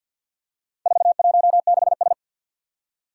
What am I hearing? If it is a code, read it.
Morse code, heard as a keyed tone, about 52 wpm, 700 Hz. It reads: 416S
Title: Four Challenging Letter-Number Combinations 50wpm